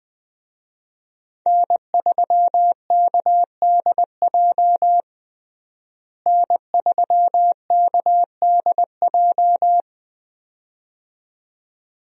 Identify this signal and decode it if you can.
Morse code, heard as a keyed tone, about 20 wpm, 700 Hz. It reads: N3KDJ N3KDJ
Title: USA Call Signs - 5 Characters - Repeated 2x - 20wpm